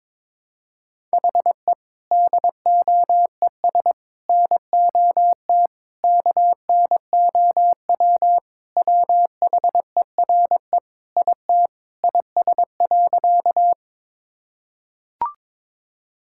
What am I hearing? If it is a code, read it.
Morse code, heard as a keyed tone, about 22 wpm, 700 Hz. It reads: HE DOES NOT KNOW WHERE IT IS.
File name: practice / 22wpm / Sentences with 4-Word Phrases - 22wpm